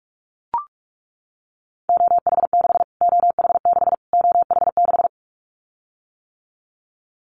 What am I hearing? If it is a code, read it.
Morse code, heard as a keyed tone, about 44 wpm, 700 Hz. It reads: OH6 OH6 OH6
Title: Callsign Prefixes for Top 25 DXCC - Repeated 3x - 45wpm